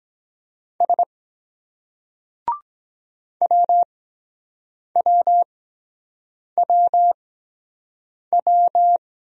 Morse code, heard as a keyed tone, about 23 wpm, spaced as for 7 wpm, 700 Hz. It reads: S WWWW